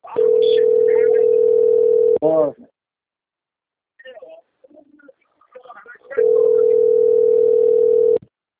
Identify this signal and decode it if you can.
Ringback tone (North America), 440 + 480 Hz, with speech.
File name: rg-1600-unknown- - - .43848